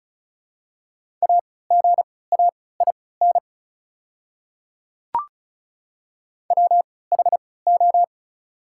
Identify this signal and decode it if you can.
Morse code, heard as a keyed tone, about 34 wpm, spaced as for 20 wpm, 700 Hz. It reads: AGAIN WHO
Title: 3x Character Spacing - Top 300 Words - 35wpm